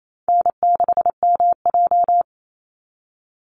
Morse code, heard as a keyed tone, about 28 wpm, 700 Hz. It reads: N6MJ
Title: Easy Call Signs - 28wpm